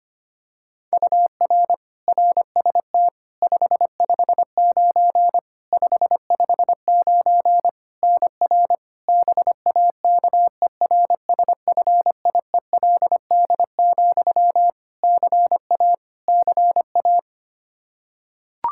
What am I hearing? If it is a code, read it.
Morse code, heard as a keyed tone, about 25 wpm, 700 Hz. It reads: UR RST 559 559 NR BAKERSFIELD, CA CA
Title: Practice QSO - Exchange 1 - Beginning - 25wpm